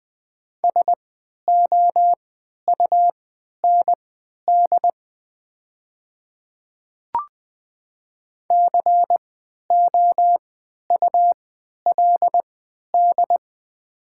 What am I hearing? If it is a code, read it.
Morse code, heard as a keyed tone, about 20 wpm, spaced as for 11 wpm, 700 Hz. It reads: SOUND COULD
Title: 3x Character Spacing - Top 100 Words - 20wpm